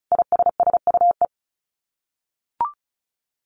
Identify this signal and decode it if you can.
Morse code, heard as a keyed tone, about 34 wpm, 700 Hz. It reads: ISSUE
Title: Sets of 5 Words Encore2 35wpm